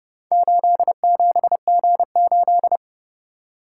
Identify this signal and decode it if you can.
Morse code, heard as a keyed tone, about 31 wpm, 700 Hz. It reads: 87G8